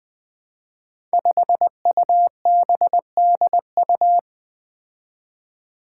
Morse code, heard as a keyed tone, about 20 wpm, 700 Hz. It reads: 5UBDU
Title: Five Challenging Letter-Number Combinations 20wpm